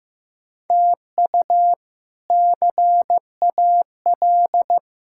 Morse code, heard as a keyed tone, about 15 wpm, 700 Hz. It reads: TU CAL
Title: K1USN SST QSO Practice - 15wpm